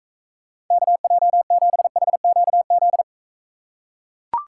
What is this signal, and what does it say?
Morse code, heard as a keyed tone, about 43 wpm, 700 Hz. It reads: KJ7HQZ